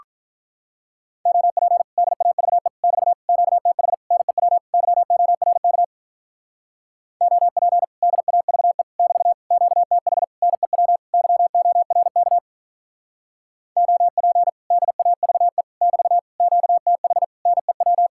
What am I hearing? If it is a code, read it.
Morse code, heard as a keyed tone, about 44 wpm, 700 Hz. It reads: OP DAVE <BT> QTH NEW YORK OP DAVE <BT> QTH NEW YORK OP DAVE <BT> QTH NEW